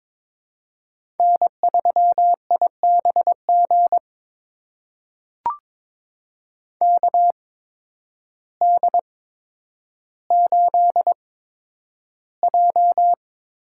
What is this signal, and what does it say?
Morse code, heard as a keyed tone, about 23 wpm, spaced as for 10 wpm, 700 Hz. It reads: N3IBG KD8J
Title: USA Call Signs - 8x Character Spacing - 22wpm